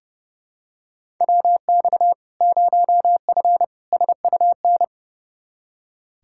Morse code, heard as a keyed tone, about 32 wpm, 700 Hz. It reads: WX 0F SUN
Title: Practice QSO - Exchange 2 - Info - 30wpm